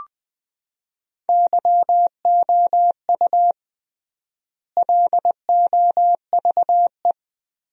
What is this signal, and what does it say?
Morse code, heard as a keyed tone, about 20 wpm, 700 Hz. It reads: YOU LOVE
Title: 3x Word-Spacing - Sentences from Top 400 Words - 20wpm